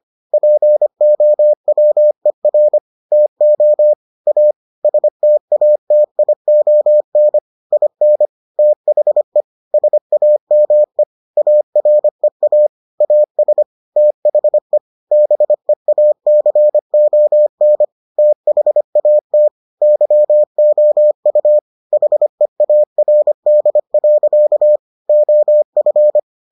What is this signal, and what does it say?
Morse code, heard as a keyed tone, about 25 wpm, 600 Hz. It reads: POWER TO A STATION IN THE SAME AREA AS THE BEACON THAT YOU HEARD. OF